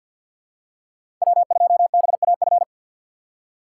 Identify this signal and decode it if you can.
Morse code, heard as a keyed tone, about 51 wpm, 700 Hz. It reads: W2BAF
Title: USA Call Signs - 5 Characters - 4x Character Spacing - 50wpm